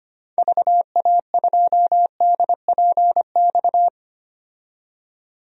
Morse code, heard as a keyed tone, about 25 wpm, 700 Hz. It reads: VA2DPX
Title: DXCC-100 Call Signs - 6 Characters - 25wpm